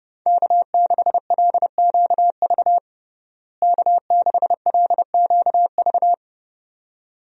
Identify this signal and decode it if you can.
Morse code, heard as a keyed tone, about 30 wpm, 700 Hz. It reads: K6LQV K6LQV
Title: USA Call Signs - 5 Characters - Repeated 2x - 30wpm